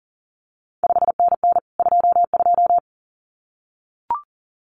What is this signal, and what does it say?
Morse code, heard as a keyed tone, about 42 wpm, 700 Hz. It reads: 5NN 22
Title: CQ WW DX Contest Exchange Encore - 40wpm